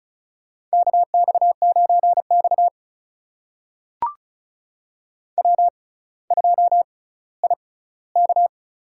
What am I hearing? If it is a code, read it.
Morse code, heard as a keyed tone, about 36 wpm, spaced as for 18 wpm, 700 Hz. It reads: KX9X W2IK
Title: USA Call Signs - 6x Character Spacing - 35wpm